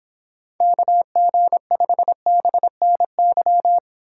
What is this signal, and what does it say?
Morse code, heard as a keyed tone, about 26 wpm, 700 Hz. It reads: KG5BNY